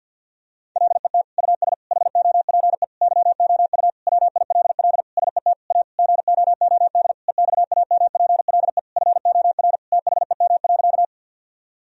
Morse code, heard as a keyed tone, about 52 wpm, 700 Hz. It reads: LET US HOPE YOU WILL SET A GOOD EXAMPLE FOR THEM.
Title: Sentences from Top 1000 Words - Review Part 1 - 50wpm